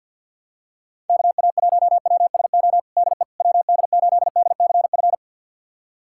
Morse code, heard as a keyed tone, about 49 wpm, 700 Hz. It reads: KA1WSO DE WD8DCF
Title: USA Callsign DE Callsign - 6 Characters - 50wpm